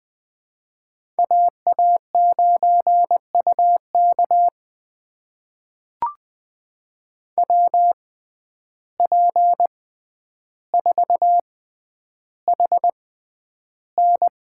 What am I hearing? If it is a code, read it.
Morse code, heard as a keyed tone, about 20 wpm, spaced as for 10 wpm, 700 Hz. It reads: AA9UK WP4HN
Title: USA Call Signs - 5 Characters - 6x Character Spacing - 20wpm